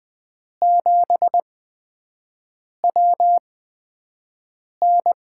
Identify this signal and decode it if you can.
Morse code, heard as a keyed tone, about 20 wpm, spaced as for 5 wpm, 700 Hz. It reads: 7WN
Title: USA Call Signs - 8x Character Spacing - 20wpm